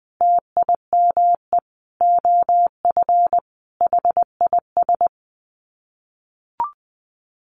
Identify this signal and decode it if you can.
Morse code, heard as a keyed tone, about 20 wpm, 700 Hz. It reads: TIME OF HIS
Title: Sets of 5 Words Encore2 20wpm